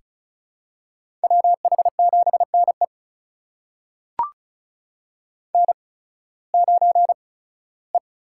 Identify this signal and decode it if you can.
Morse code, heard as a keyed tone, about 36 wpm, spaced as for 19 wpm, 700 Hz. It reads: WH7NE N9E
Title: USA Call Signs - 5 Characters - 8x Character Spacing - 35wpm